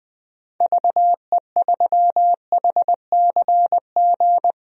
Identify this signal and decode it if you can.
Morse code, heard as a keyed tone, about 20 wpm, 700 Hz. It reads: VE3HCG
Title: DXCC-100 Call Signs - 6 Characters - Repeated 3x - 20wpm